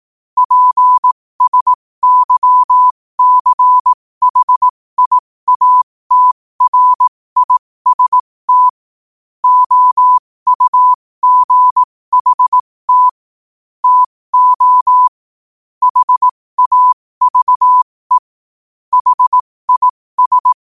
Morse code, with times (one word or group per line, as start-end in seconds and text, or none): 0.4-8.7 PSYCHIATRIST
9.4-13.1 OUGHT
13.8-15.1 TO
15.8-18.2 HAVE
18.9-20.5 HIS